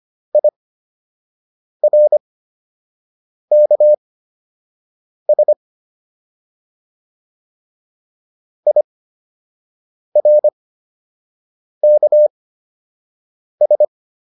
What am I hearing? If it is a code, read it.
Morse code, heard as a keyed tone, about 25 wpm, spaced as for 6 wpm, 600 Hz. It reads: IRKS IRKS